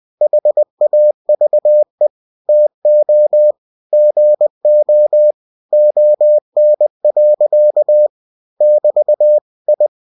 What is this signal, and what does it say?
Morse code, heard as a keyed tone, about 20 wpm, 600 Hz. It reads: HAVE TO GO ON. <BT> I
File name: cw-books / winnie-the-pooh / winnie-20wpm-page-0016